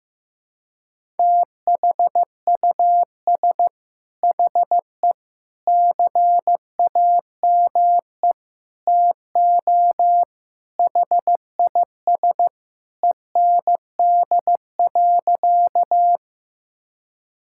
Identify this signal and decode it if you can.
Morse code, heard as a keyed tone, about 15 wpm, 700 Hz. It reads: THUS HE CAME TO HIS END.
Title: Sentences from Top 1000 Words - Review Part 5 - 15wpm